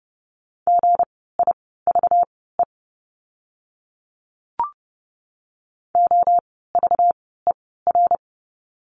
Morse code, heard as a keyed tone, about 30 wpm, spaced as for 17 wpm, 700 Hz. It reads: GIVE OVER
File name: course / 30:10wpm / 230 Lesson 37f - Letter x - Words 30wpm